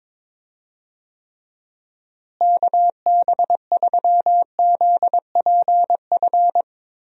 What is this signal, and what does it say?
Morse code, heard as a keyed tone, about 22 wpm, 700 Hz. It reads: KB3ZPF
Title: USA Call Signs - 6 Characters - Repeated 3x - 22wpm